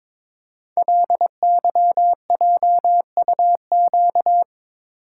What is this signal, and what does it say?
Morse code, heard as a keyed tone, about 22 wpm, 690 Hz. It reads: LYJUQ
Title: Code Groups - 5 Letters - 22wpm